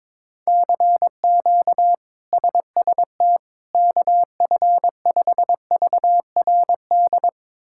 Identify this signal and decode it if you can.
Morse code, heard as a keyed tone, about 22 wpm, 700 Hz. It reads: CQ SST KF5VRD
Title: K1USN SST QSO Practice - 22wpm